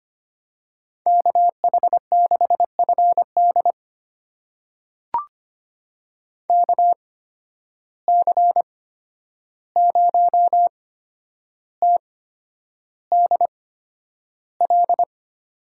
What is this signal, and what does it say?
Morse code, heard as a keyed tone, about 25 wpm, spaced as for 10 wpm, 700 Hz. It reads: KH6FD KC0TDL
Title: USA Call Signs - 8x Character Spacing - 25wpm